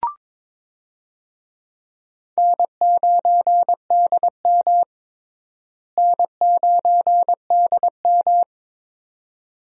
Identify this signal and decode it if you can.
Morse code, heard as a keyed tone, about 22 wpm, 700 Hz. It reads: N9DM N9DM
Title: USA Call Signs - 4 Characters - Repeated 2x - 22wpm